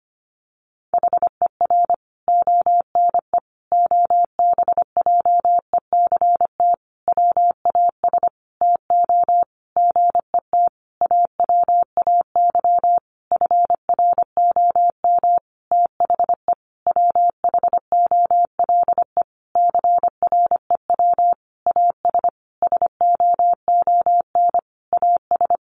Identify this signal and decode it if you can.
Morse code, heard as a keyed tone, about 25 wpm, 700 Hz. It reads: HER ONE OBJECT WAS TO GET AWAY FROM THE WHOLE CREW AS SOON AS
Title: Sentences from Top 1500 Words 25wpm